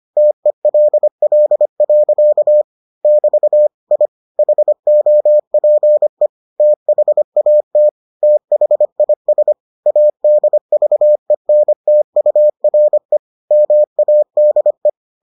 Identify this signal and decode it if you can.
Morse code, heard as a keyed tone, about 25 wpm, 600 Hz. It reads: TELL. <BT> I HOPE THAT THIS ADVENTURE MADE